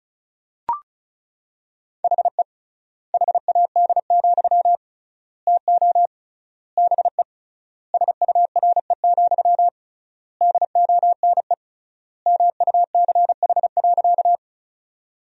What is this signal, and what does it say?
Morse code, heard as a keyed tone, about 34 wpm, 700 Hz. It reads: HE HAD, TO BE SURE, DONE MUCH.
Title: 3x Word-Spacing - Sentences from Top 500 Words - 35wpm